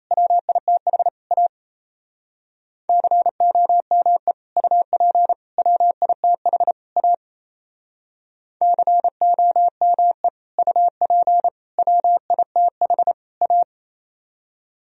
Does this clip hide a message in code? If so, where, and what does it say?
Yes: Morse code from 0.1 to 13.6 s, heard as a keyed tone, about 37 wpm, 700 Hz. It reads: WITH A COME UP WITH A COME UP WITH A